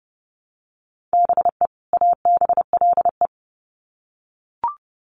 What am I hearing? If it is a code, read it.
Morse code, heard as a keyed tone, about 29 wpm, 700 Hz. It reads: BE ABLE
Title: Sets of 2 Words 30wpm